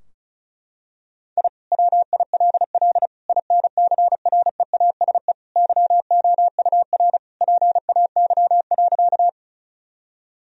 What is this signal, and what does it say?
Morse code, heard as a keyed tone, about 36 wpm, 700 Hz. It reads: I WILL INCREASE YOUR PAY.
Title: Sentences from Top 1000 Words - Review Part 4 - 35wpm